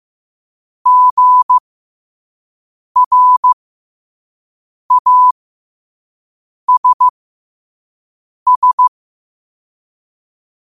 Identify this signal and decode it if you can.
Morse code, heard as a keyed tone, about 15 wpm, spaced as for 5 wpm, 1000 Hz. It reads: GRASS
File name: qotdfiles / 5wpm